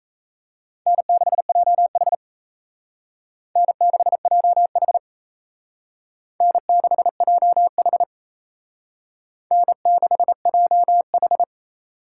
Morse code, heard as a keyed tone, about 36 wpm, 700 Hz. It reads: N6JH N6JH N6JH N6JH